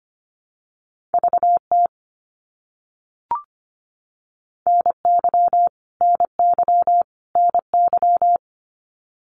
Morse code, heard as a keyed tone, about 25 wpm, 700 Hz. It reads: VT NY NY NY
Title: US States Repeated 3x 25wpm